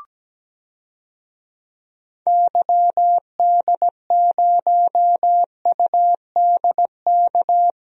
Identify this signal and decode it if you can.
Morse code, heard as a keyed tone, about 17 wpm, 700 Hz. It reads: YD0UDK
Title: DXCC-100 Call Signs - 6 Characters - Repeated 3x - 17wpm